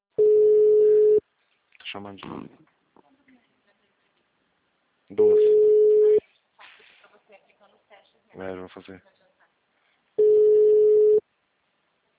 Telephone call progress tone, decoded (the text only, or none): ringback tone